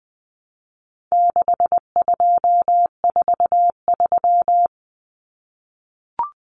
Morse code, heard as a keyed tone, about 20 wpm, 700 Hz. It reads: 6243
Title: Numbers - 4-Digit 20wpm